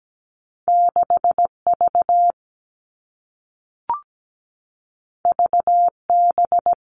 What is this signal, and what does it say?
Morse code, heard as a keyed tone, about 17 wpm, 700 Hz. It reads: 6V VB
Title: Two Challenging Letter-Number Combinations 17wpm